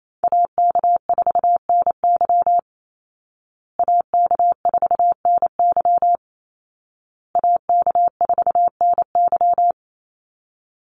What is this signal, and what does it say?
Morse code, heard as a keyed tone, about 28 wpm, 700 Hz. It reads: AK4NY AK4NY AK4NY
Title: Call Signs Repeated 2x - 28wpm